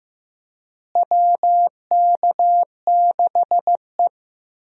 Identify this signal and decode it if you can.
Morse code, heard as a keyed tone, about 15 wpm, 700 Hz. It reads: WK6E
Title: USA Call Signs - 4 Characters - Repeated 3x - 15wpm